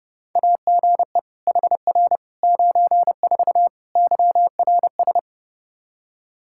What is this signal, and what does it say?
Morse code, heard as a keyed tone, about 30 wpm, 700 Hz. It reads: AGE HR 94 YRS
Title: K1USN SST QSO Practice 30wpm